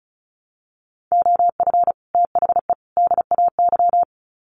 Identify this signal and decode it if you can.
Morse code, heard as a keyed tone, about 34 wpm, 700 Hz. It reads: OF THE DAY
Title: Sets of 3 Words 35wpm